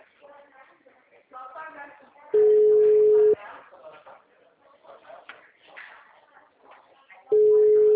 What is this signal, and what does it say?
Ringback tone (Europe, 425 Hz), with speech.
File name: out- -2010- - - .13711